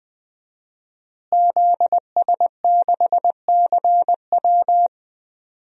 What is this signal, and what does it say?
Morse code, heard as a keyed tone, about 20 wpm, 700 Hz. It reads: ZS6CW
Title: DXCC-100 Call Signs - 5 Characters - Repeated 2x - 20wpm